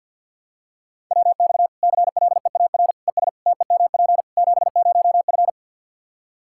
Morse code, heard as a keyed tone, about 53 wpm, 700 Hz. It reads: WX CLEAR ES TEMP 70F
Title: Practice QSO - Exchange 2 - Info - 50wpm